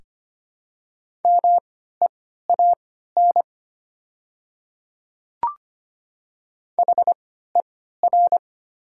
Morse code, heard as a keyed tone, about 25 wpm, spaced as for 14 wpm, 690 Hz. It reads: MEAN HER
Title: 3x Character Spacing - Top 200 Words - 25wpm